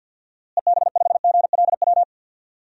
Morse code, heard as a keyed tone, about 52 wpm, 700 Hz. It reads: EB5GLW
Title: DXCC-100 Call Signs - 6 Characters - Repeated 3x - 50wpm